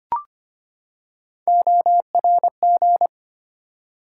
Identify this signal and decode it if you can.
Morse code, heard as a keyed tone, about 25 wpm, 700 Hz. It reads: ORG
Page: morse-code-ninja-practice-25wpm